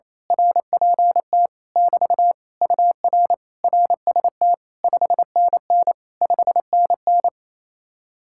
Morse code, heard as a keyed tone, about 26 wpm, 700 Hz. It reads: RPT <BT> UR RST 5NN 5NN
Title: Speed-Racing Common QSO Phrases - 28wpm